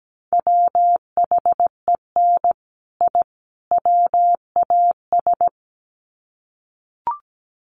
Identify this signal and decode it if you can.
Morse code, heard as a keyed tone, about 17 wpm, 700 Hz. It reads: WHEN I WAS
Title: Sets of 3 Words 17wpm